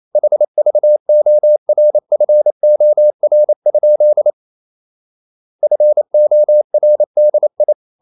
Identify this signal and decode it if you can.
Morse code, heard as a keyed tone, about 28 wpm, 600 Hz. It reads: HVORFOR? FORDI
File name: Ferie QRQ 28wpm